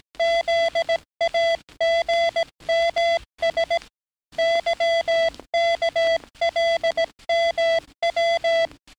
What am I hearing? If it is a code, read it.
Morse code, heard as a keyed tone, about 18 wpm, 670 Hz. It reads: ZAGMS YKLMW
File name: training / NAVPERS 11049to11060 / NAVPERS 11058 RA RB